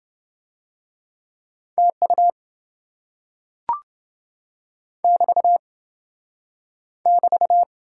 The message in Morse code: TU <BT> <BT>

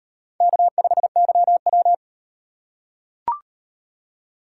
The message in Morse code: K5YW

